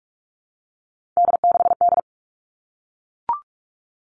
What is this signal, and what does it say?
Morse code, heard as a keyed tone, about 44 wpm, 700 Hz. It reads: D6D
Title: All Cut Numbers - 3-Digit 45wpm